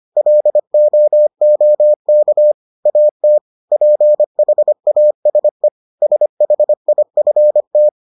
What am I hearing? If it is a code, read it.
Morse code, heard as a keyed tone, about 25 wpm, 600 Hz. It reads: LOOK AT PHASE SHIFT